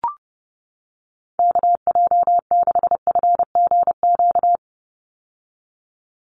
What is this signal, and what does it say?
Morse code, heard as a keyed tone, about 30 wpm, 700 Hz. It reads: KJ6FGQ
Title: USA Call Signs 30wpm